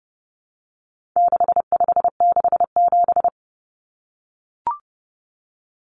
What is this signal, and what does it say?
Morse code, heard as a keyed tone, about 30 wpm, 700 Hz. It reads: 6567
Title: Numbers - 4-Digit 30wpm